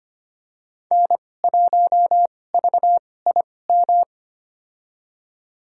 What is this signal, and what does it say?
Morse code, heard as a keyed tone, about 25 wpm, spaced as for 18 wpm, 700 Hz. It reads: N1VIM
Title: USA Call Signs - 5 Characters - 2x Character Spacing - 25wpm